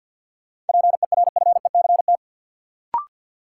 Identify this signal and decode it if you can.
Morse code, heard as a keyed tone, about 53 wpm, 700 Hz. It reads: PERFECT
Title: Sentences from Top 1000 Words - New Word Drill - 50wpm